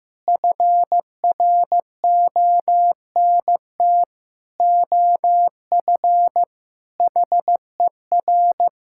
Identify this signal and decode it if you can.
Morse code, heard as a keyed tone, about 15 wpm, 700 Hz. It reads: FRONT OF HER
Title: Sentences from Top 1000 Words - Review Part 5 - 15wpm